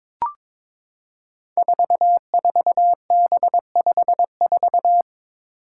Morse code, heard as a keyed tone, about 22 wpm, 700 Hz. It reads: 44B54
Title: Five Challenging Letter-Number Combinations 22wpm